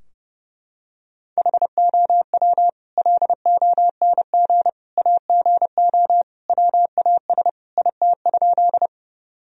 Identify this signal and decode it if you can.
Morse code, heard as a keyed tone, about 30 wpm, 700 Hz. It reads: HOW LONG AGO WAS IT?
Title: Sentences from Top 1000 Words - Review Part 3 - 30wpm